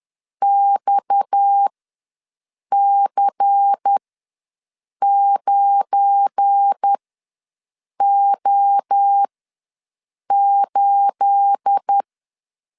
Morse code, heard as a keyed tone, about 10 wpm, spaced as for 6 wpm, 800 Hz. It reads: XC9O8
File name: LEZIONI / 9 28 ws 10wpm 4124